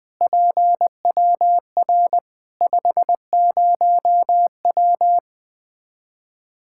Morse code, heard as a keyed tone, about 20 wpm, 700 Hz. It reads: PWR 50W
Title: Practice QSO - Exchange 2 - Info - 20wpm